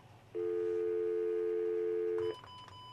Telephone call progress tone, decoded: dial tone